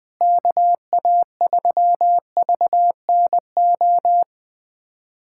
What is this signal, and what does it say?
Morse code, heard as a keyed tone, about 21 wpm, 700 Hz. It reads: KA3VNO
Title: USA Callsign DE Callsign - 6 Characters - 20wpm